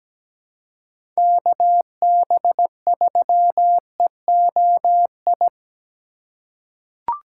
Morse code, heard as a keyed tone, about 17 wpm, 700 Hz. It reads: KB3EOI
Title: USA Call Signs - 6 Characters - 4x Character Spacing - 17wpm